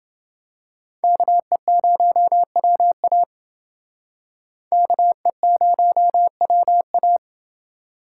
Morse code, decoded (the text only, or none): KE0WA KE0WA